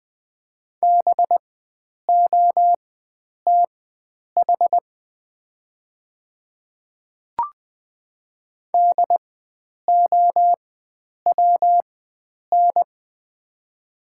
Morse code, heard as a keyed tone, about 20 wpm, spaced as for 9 wpm, 700 Hz. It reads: BOTH DOWN
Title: 4x Character Spacing - Top 300 Words - 20wpm